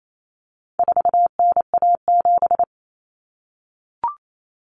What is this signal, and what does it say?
Morse code, heard as a keyed tone, about 28 wpm, 700 Hz. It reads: 4NA7
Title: Common Cut Numbers - 4-Digit 28wpm